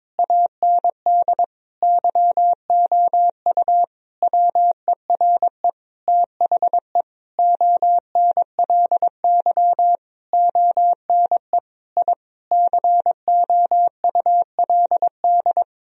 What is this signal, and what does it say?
Morse code, heard as a keyed tone, about 22 wpm, 700 Hz. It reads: AND YOU WERE THE ONLY ONE I COULD
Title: Sentences from Top 1000 Words - Review Part 4 - 22wpm